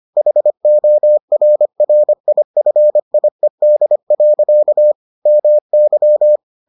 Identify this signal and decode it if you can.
Morse code, heard as a keyed tone, about 25 wpm, 600 Hz. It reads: HORRIFIED. MY